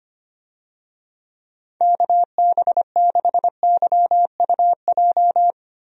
Morse code, heard as a keyed tone, about 25 wpm, 700 Hz. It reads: KB6YUJ
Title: USA Call Signs - 6 Characters - Repeated 2x - 25wpm